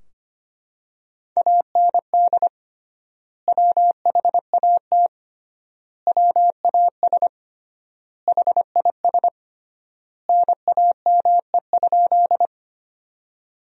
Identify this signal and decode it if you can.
Morse code, heard as a keyed tone, about 25 wpm, 700 Hz. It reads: AND WHAT WAS HIS NAME?
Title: 3x Word-Spacing - Sentences from Top 200 Words - 25wpm